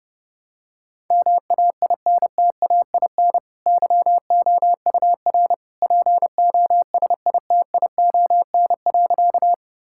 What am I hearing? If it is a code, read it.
Morse code, heard as a keyed tone, about 30 wpm, 700 Hz. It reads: MAINTAIN YOUR POSITION.
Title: Sentences from Top 1000 Words - Review Part 5 - 30wpm